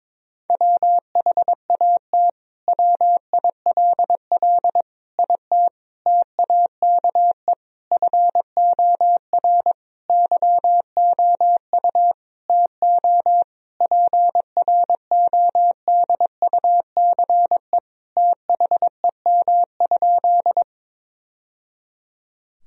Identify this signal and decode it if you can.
Morse code, heard as a keyed tone, about 22 wpm, 700 Hz. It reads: WHAT WILL IT TAKE FOR YOU TO PRODUCE THEM?
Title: Sentences from Top 1000 Words - Review Part 3 - 22wpm